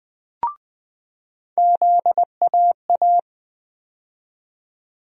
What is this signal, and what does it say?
Morse code, heard as a keyed tone, about 20 wpm, 700 Hz. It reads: ZAA
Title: USA Call Sign Postfixes - 3 Letters 20wpm